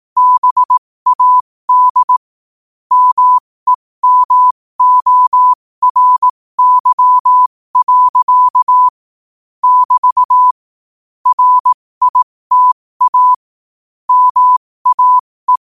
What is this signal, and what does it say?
Morse code, heard as a keyed tone, about 18 wpm, spaced as for 15 wpm, 1000 Hz. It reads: BAD MEMORY. <BT> RITA MAE
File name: 15wpm